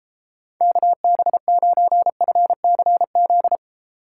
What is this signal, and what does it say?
Morse code, heard as a keyed tone, about 32 wpm, 700 Hz. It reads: KB9FCZ